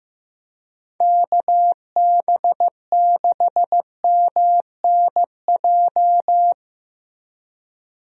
Morse code, heard as a keyed tone, about 15 wpm, 700 Hz. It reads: KB6MNJ
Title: USA Call Signs - 6 Characters - Repeated 2x - 15wpm